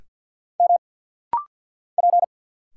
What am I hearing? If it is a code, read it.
Morse code, heard as a keyed tone, about 50 wpm, 700 Hz. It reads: M P
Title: Single Letters - Rapid-Fire 50wpm